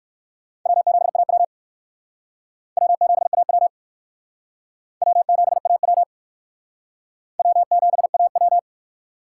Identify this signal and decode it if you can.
Morse code, heard as a keyed tone, about 55 wpm, 700 Hz. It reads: W7AW W7AW W7AW W7AW
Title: Speed Racing - USA 1x2 Call Signs 45wpm